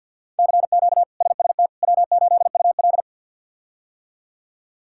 Morse code, heard as a keyed tone, about 52 wpm, 700 Hz. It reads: CQ SST W8UL
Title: K1USN SST QSO Practice - 50wpm